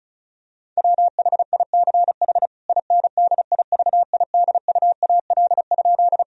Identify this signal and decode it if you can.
Morse code, heard as a keyed tone, about 34 wpm, 700 Hz. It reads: WHICH INDIVIDUAL?
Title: Sentences from Top 1000 Words - Review Part 4 - 35wpm